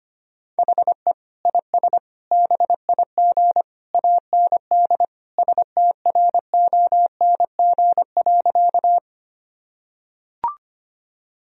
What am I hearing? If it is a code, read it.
Morse code, heard as a keyed tone, about 25 wpm, 700 Hz. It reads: HE IS BIG AND STRONG.
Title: Sentences from Top 1000 Words - Review Part 3 - 25wpm